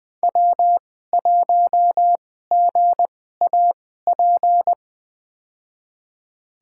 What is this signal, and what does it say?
Morse code, heard as a keyed tone, about 20 wpm, spaced as for 14 wpm, 700 Hz. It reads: W1GAP
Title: USA Call Signs - 2x Character Spacing - 20wpm